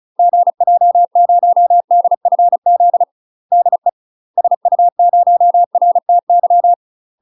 Morse code, heard as a keyed tone, about 34 wpm, 700 Hz. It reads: GJ0DFZ DE SU0RTY
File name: UK MASTER QSOs TRAINING / 35WPM / UK MASTER QSOs 0003